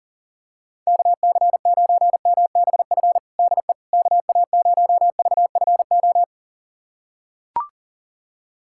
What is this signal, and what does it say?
Morse code, heard as a keyed tone, about 40 wpm, 700 Hz. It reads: KC9MBF DE KA0VFO
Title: USA Callsign DE Callsign - 6 Characters - 40wpm